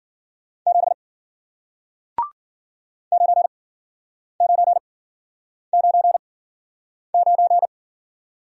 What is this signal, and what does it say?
Morse code, heard as a keyed tone, about 52 wpm, spaced as for 9 wpm, 700 Hz. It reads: 7 9999